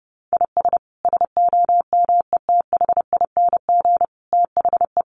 Morse code, heard as a keyed tone, about 30 wpm, 700 Hz. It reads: IS SOMETHING THE
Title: Sentences from Top 700 Words 30wpm